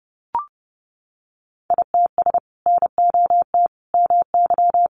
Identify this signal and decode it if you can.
Morse code, heard as a keyed tone, about 30 wpm, 700 Hz. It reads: ITS NOT MY
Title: Sentences from Top 600 Words 30wpm